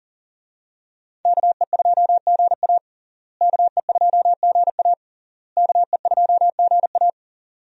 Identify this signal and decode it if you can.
Morse code, heard as a keyed tone, about 37 wpm, 700 Hz. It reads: KE2GA KE2GA KE2GA
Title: USA Call Signs - 5 Characters - Repeated 3x - 40wpm